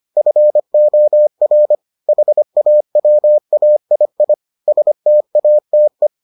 Morse code, heard as a keyed tone, about 25 wpm, 600 Hz. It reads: FOR HAWAII STATE